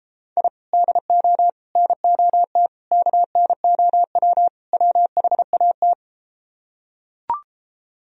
Morse code, heard as a keyed tone, about 33 wpm, 700 Hz. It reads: I DO NOT KNOW WHAT